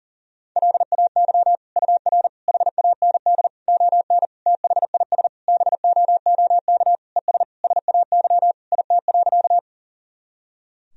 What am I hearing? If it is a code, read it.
Morse code, heard as a keyed tone, about 40 wpm, 700 Hz. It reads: LAY UR HAND ON THIS BOOK ES SAY IT.
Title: Sentences from Top 400 Words and Abbreviations 40wpm